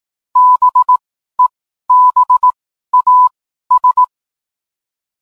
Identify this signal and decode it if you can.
Morse code, heard as a keyed tone, about 18 wpm, spaced as for 12 wpm, 1000 Hz. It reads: BEBAS